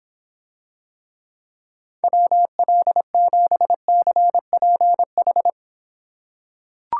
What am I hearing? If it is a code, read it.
Morse code, heard as a keyed tone, about 26 wpm, 700 Hz. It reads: WL7CPH